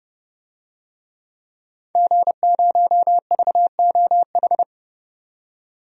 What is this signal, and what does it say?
Morse code, heard as a keyed tone, about 31 wpm, 700 Hz. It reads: G0VOH
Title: DXCC-100 Call Signs - 5 Characters - 30wpm